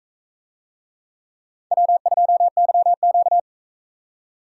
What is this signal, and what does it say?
Morse code, heard as a keyed tone, about 42 wpm, 700 Hz. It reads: W2YQ